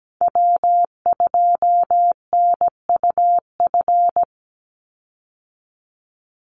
Morse code, heard as a keyed tone, about 17 wpm, 700 Hz. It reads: W2NUF
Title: USA Call Signs 17wpm